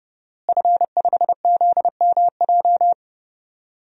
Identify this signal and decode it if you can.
Morse code, heard as a keyed tone, about 32 wpm, 700 Hz. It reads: F5ZMJ
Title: DXCC-100 Call Signs - 5 Characters - Repeated 2x - 30wpm